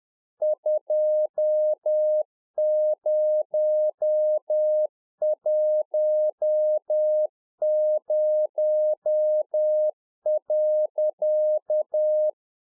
Morse code, heard as a keyed tone, about 10 wpm, 610 Hz. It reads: 2010.
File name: 2011-04-mp3 / A SMALL TRANCEIVER 1 10WPM